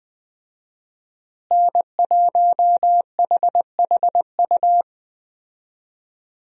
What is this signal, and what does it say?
Morse code, heard as a keyed tone, about 20 wpm, 700 Hz. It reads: N1HHU
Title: USA Call Signs - 5 Characters - Repeated 3x - 20wpm